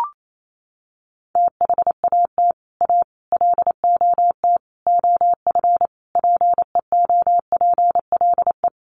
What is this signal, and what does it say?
Morse code, heard as a keyed tone, about 28 wpm, 700 Hz. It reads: THAT A LOT OF PEOPLE
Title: Sets of 5 Words 28wpm